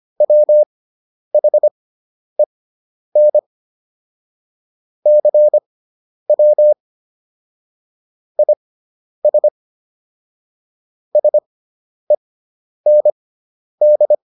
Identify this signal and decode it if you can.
Morse code, heard as a keyed tone, about 25 wpm, spaced as for 10 wpm, 600 Hz. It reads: WHEN CW IS SEND